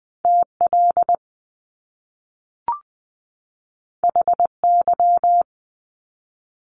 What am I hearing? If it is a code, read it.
Morse code, heard as a keyed tone, about 20 wpm, 700 Hz. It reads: TL HY
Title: Top Two Letter Combinations in Words 20wpm